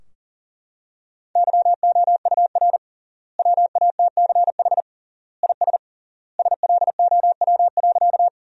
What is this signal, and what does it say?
Morse code, heard as a keyed tone, about 40 wpm, 700 Hz. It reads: YOUR WATCH IS SLOW.
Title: 3x Word-Spacing - Sentences from Top 400 Words - 40wpm